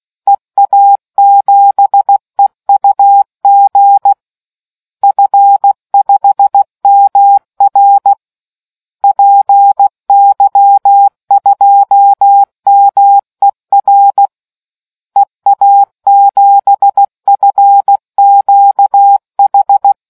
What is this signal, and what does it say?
Morse code, heard as a keyed tone, about 16 wpm, 800 Hz. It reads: EA7EUG F5MR PY2MER EA7FQH